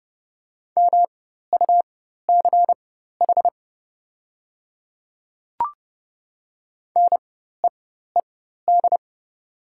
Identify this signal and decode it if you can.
Morse code, heard as a keyed tone, about 31 wpm, spaced as for 14 wpm, 700 Hz. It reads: MUCH NEED
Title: 4x Character Spacing - Top 200 Words - 30wpm